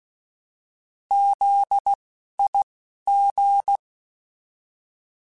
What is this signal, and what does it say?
Morse code, heard as a keyed tone, about 16 wpm, spaced as for 11 wpm, 800 Hz. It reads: ZIG
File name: Alphabetical / Alphabetical mp3s / Zs @10